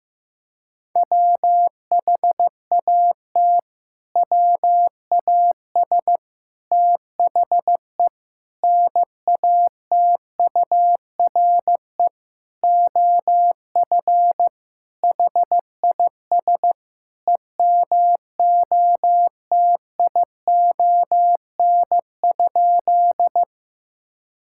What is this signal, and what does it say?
Morse code, heard as a keyed tone, about 15 wpm, 700 Hz. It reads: WHAT WAS THE NATURE OF HIS EMOTION?
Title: Sentences from Top 1600 Words 15wpm